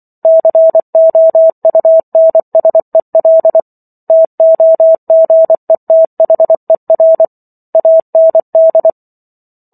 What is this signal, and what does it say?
Morse code, heard as a keyed tone, about 24 wpm, 650 Hz. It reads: COUNSEL TOGETHER AND
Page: War of the Worlds - 15-Chapter 15 - 24 WPM